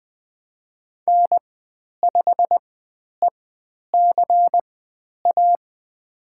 Morse code, heard as a keyed tone, about 20 wpm, spaced as for 10 wpm, 700 Hz. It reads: N5ECA
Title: 095 Lesson 16g - Letter c - Call Signs 20wpm